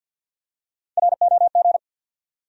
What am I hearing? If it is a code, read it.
Morse code, heard as a keyed tone, about 52 wpm, 700 Hz. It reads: ROG
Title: USA Common Names - Max 3 Letters - 50wpm